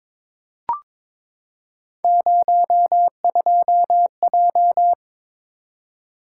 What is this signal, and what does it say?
Morse code, heard as a keyed tone, about 21 wpm, 700 Hz. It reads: 02J
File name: practice / 22wpm / Three Letter-Number Combinations - 22wpm